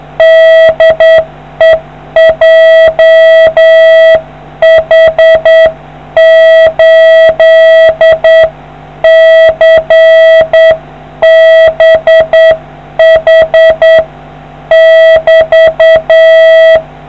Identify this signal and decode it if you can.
Morse code, heard as a keyed tone, about 8 wpm, 650 Hz. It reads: DE JH8CBH <BT>